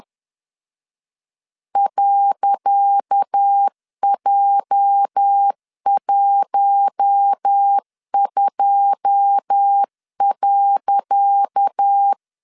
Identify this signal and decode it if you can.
Morse code, heard as a keyed tone, about 10 wpm, 800 Hz. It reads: .J12.
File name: LEZIONI / 10 SIGNS / 10C DOT 10wpm 2940